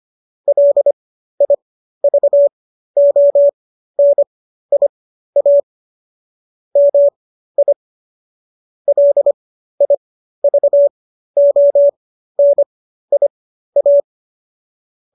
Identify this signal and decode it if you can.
Morse code, heard as a keyed tone, about 25 wpm, spaced as for 13 wpm, 570 Hz. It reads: LIVONIA MI LIVONIA